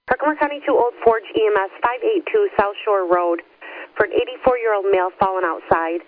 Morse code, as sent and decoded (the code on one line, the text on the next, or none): none